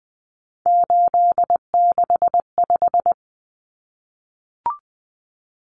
Morse code, heard as a keyed tone, about 20 wpm, 700 Hz. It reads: 865
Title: Common Cut Numbers - 3-Digit 20wpm